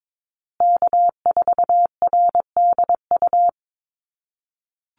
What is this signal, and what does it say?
Morse code, heard as a keyed tone, about 22 wpm, 690 Hz. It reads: K4RDU
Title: USA Call Signs Repeated 3x 22wpm